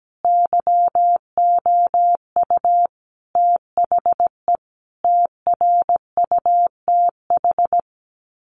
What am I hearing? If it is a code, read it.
Morse code, heard as a keyed tone, about 17 wpm, 700 Hz. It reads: YOU THE TRUTH
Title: Sets of 5 Words 17wpm